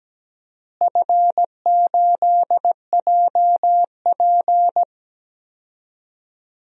Morse code, heard as a keyed tone, about 17 wpm, 700 Hz. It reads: F8JP
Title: DXCC-100 Call Signs - 4 Characters - Repeated 2x - 17wpm